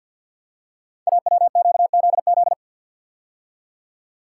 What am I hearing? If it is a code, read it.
Morse code, heard as a keyed tone, about 50 wpm, 700 Hz. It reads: AWQZZ
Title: Code Groups - 5 Letters - 50wpm